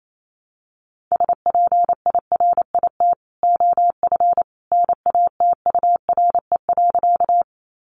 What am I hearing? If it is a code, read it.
Morse code, heard as a keyed tone, about 28 wpm, 700 Hz. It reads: SPIRIT OF NATURE.
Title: Sentences from Top 1400 Words 28wpm